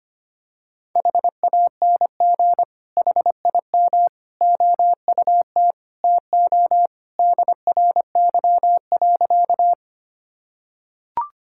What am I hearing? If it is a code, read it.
Morse code, heard as a keyed tone, about 25 wpm, 700 Hz. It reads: HANG HIM OUT TO DRY.
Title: Sentences from Top 1600 Words 25wpm